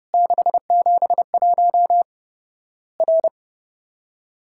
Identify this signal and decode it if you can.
Morse code, heard as a keyed tone, about 32 wpm, 700 Hz. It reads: 671 R
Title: Worked All Europe DX Contest - QTC Exchange - 30wpm